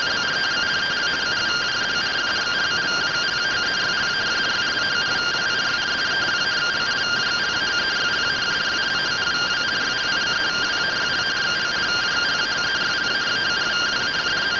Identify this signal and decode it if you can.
Two-tone teleprinter signal, about 170 Hz shift.